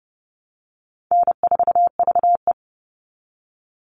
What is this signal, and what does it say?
Morse code, heard as a keyed tone, about 30 wpm, 700 Hz. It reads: N4VE
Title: All Cut Numbers - 4-Digit 30wpm